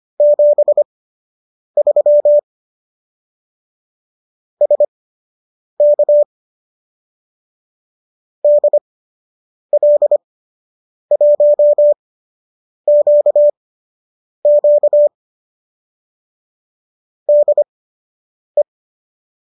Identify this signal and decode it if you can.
Morse code, heard as a keyed tone, about 25 wpm, spaced as for 8 wpm, 590 Hz. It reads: 73 SK DL1QQ DE